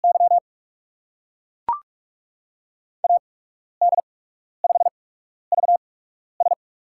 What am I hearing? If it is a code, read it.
Morse code, heard as a keyed tone, about 42 wpm, spaced as for 12 wpm, 700 Hz. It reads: Y AD5VS
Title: USA Call Signs - 5 Characters - 8x Character Spacing - 45wpm